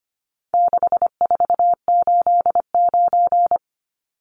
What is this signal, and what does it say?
Morse code, heard as a keyed tone, about 25 wpm, 700 Hz. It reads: 6489
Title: Numbers - 4-Digit 25wpm